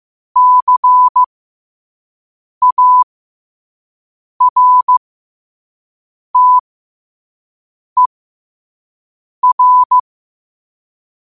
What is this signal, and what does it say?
Morse code, heard as a keyed tone, about 15 wpm, spaced as for 5 wpm, 1000 Hz. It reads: CARTER
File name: qotdfiles / 5wpm